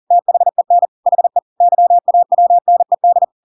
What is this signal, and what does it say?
Morse code, heard as a keyed tone, about 40 wpm, 700 Hz. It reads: THEN HE YAWNED
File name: Pinocchio / Pinocchio ch7 40 40 ik1hge